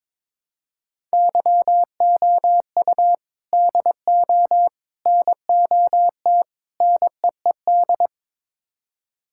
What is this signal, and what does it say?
Morse code, heard as a keyed tone, about 22 wpm, 700 Hz. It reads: YOU DO NOT NEED
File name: practice / 22wpm / Build A Phrase - 5-Word Phrases - 22wpm